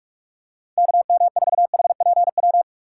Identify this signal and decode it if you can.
Morse code, heard as a keyed tone, about 47 wpm, 700 Hz. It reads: KM4HPW
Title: USA Call Signs - 6 Characters - 3x Character Spacing - 45wpm